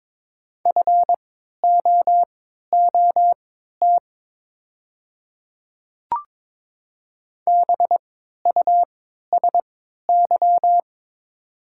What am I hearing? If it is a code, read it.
Morse code, heard as a keyed tone, about 22 wpm, spaced as for 12 wpm, 700 Hz. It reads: FOOT BUSY
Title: 3x Character Spacing - Top 500 Words - 22wpm